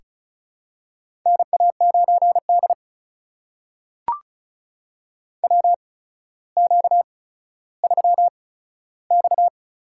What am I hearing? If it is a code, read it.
Morse code, heard as a keyed tone, about 36 wpm, spaced as for 15 wpm, 700 Hz. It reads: NA9D WQ3X
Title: USA Call Signs - 8x Character Spacing - 35wpm